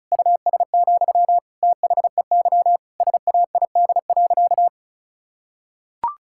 Morse code, heard as a keyed tone, about 36 wpm, 700 Hz. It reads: US, THEY SAID.
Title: Sentences from Top 1600 Words 35wpm